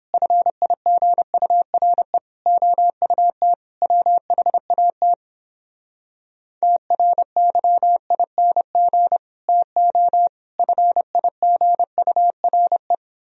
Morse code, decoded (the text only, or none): FIGURE OUT WHAT TRYING TO FIGURE